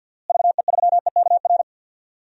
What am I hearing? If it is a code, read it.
Morse code, heard as a keyed tone, about 52 wpm, 700 Hz. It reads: VE3EKR
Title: DXCC-100 Call Signs - 6 Characters - Repeated 3x - 50wpm